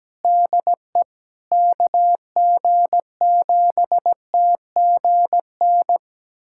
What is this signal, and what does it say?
Morse code, heard as a keyed tone, about 17 wpm, 700 Hz. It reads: DE KG7TGN
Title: USA Callsign DE Callsign 17wpm